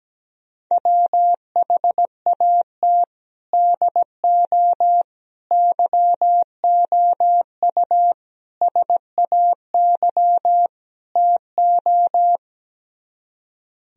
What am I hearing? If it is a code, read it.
Morse code, heard as a keyed tone, about 17 wpm, 700 Hz. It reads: WHAT DO YOU SAY TO
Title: Speed-Racing - Sets of 5 Words 17wpm